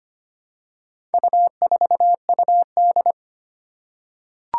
Morse code, heard as a keyed tone, about 25 wpm, 700 Hz. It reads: U4UD
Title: Four Challenging Letter-Number Combinations 25wpm